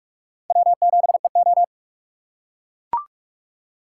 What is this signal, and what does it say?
Morse code, heard as a keyed tone, about 44 wpm, 700 Hz. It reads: W7EO